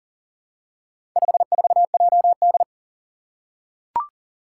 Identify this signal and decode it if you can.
Morse code, heard as a keyed tone, about 40 wpm, 700 Hz. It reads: 54JD
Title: Four Challenging Letter-Number Combinations 40wpm